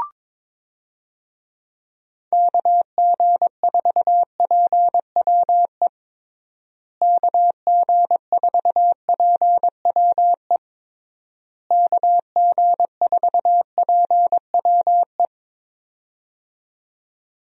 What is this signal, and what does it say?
Morse code, heard as a keyed tone, about 21 wpm, 700 Hz. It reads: KG4PWE KG4PWE KG4PWE
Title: USA Call Signs - 6 Characters - Repeated 3x - 22wpm